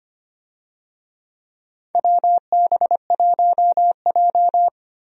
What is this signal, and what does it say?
Morse code, heard as a keyed tone, about 25 wpm, 700 Hz. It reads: WB1J
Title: USA Call Signs - 4 Characters - 25wpm